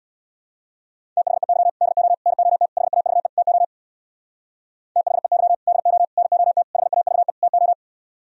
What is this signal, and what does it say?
Morse code, heard as a keyed tone, about 64 wpm, 700 Hz. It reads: THEY DO NOT HAVE TO THEY DO NOT HAVE TO